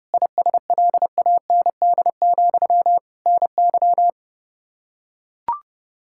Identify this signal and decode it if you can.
Morse code, heard as a keyed tone, about 32 wpm, 700 Hz. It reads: ISLAND, NY
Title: Practice QSO - Exchange 1 - Beginning - 30wpm